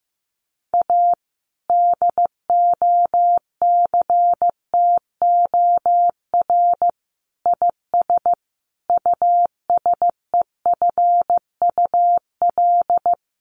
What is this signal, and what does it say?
Morse code, heard as a keyed tone, about 15 wpm, 700 Hz. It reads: A DOCTOR IS USEFUL
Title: Sentences from Top 1500 Words 15wpm